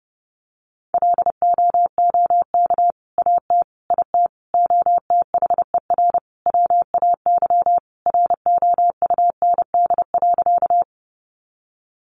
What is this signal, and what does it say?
Morse code, heard as a keyed tone, about 30 wpm, 700 Hz. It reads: LOOK AT IT OTHER WAY ROUND.
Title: Sentences from Top 200 Words 30wpm